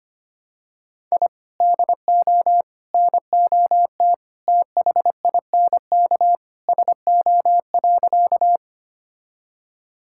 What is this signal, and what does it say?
Morse code, heard as a keyed tone, about 25 wpm, 700 Hz. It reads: I DO NOT THINK SO.
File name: practice / 25wpm / Sentences with 3-Word Phrases - 25wpm